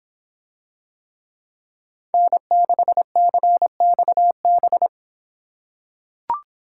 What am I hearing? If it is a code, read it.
Morse code, heard as a keyed tone, about 24 wpm, 700 Hz. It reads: N6CXB